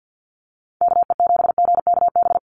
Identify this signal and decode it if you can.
Morse code, heard as a keyed tone, about 50 wpm, 700 Hz. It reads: KE7GKB
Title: USA Call Signs 50wpm